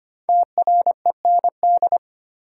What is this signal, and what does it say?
Morse code, heard as a keyed tone, about 25 wpm, 700 Hz. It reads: TREND
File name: 5 Letter Words - First 3 Letters Given - 25wpm